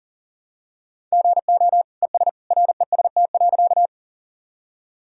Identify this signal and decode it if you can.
Morse code, heard as a keyed tone, about 40 wpm, 700 Hz. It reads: GO ES REST.
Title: Sentences from Top 500 Words and Abbreviations 40wpm